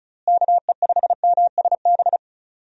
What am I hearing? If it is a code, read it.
Morse code, heard as a keyed tone, about 34 wpm, 700 Hz. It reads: KE5MSB
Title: USA Call Signs - 6 Characters - Repeated 3x - 35wpm